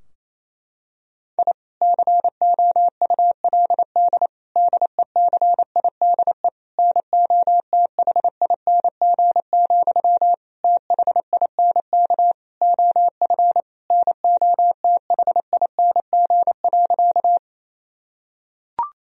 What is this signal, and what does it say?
Morse code, heard as a keyed tone, about 28 wpm, 700 Hz. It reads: I COULD DECIDE NOTHING, THINK OF NOTHING.
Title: Sentences from Top 1000 Words - Review Part 3 - 28wpm